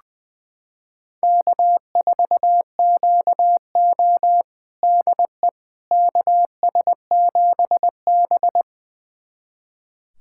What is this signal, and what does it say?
Morse code, heard as a keyed tone, about 20 wpm, 700 Hz. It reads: K4QO DE KS7B
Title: USA Callsign DE Callsign - 4 Characters - 20wpm